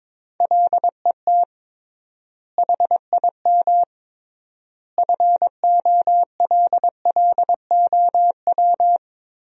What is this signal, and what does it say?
Morse code, heard as a keyed tone, about 22 wpm, 700 Hz. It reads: LET HIM FOLLOW
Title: 3x Word-Spacing - Sentences from Top 300 Words - 22wpm